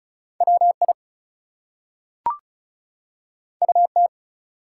Morse code, heard as a keyed tone, about 36 wpm, 700 Hz. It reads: WI UT
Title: US States and Canadian Provinces 35wpm